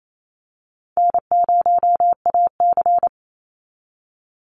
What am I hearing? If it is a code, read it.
Morse code, heard as a keyed tone, about 28 wpm, 700 Hz. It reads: N0AC
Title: Easy Call Signs - 28wpm